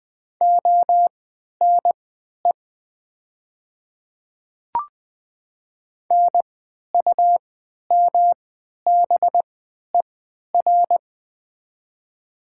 Morse code, heard as a keyed tone, about 20 wpm, spaced as for 11 wpm, 700 Hz. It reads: ONE NUMBER
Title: 3x Character Spacing - Top 300 Words - 20wpm